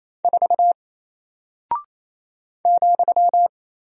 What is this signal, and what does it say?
Morse code, heard as a keyed tone, about 28 wpm, 700 Hz. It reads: 4 ,
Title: Single Character (Letter:Number:Punctuation:Prosign) Alt 28wpm